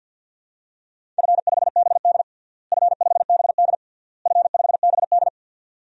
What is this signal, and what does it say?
Morse code, heard as a keyed tone, about 49 wpm, 700 Hz. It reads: F5BD F5BD F5BD
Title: DXCC-100 Call Signs - 4 Characters - Repeated 3x - 50wpm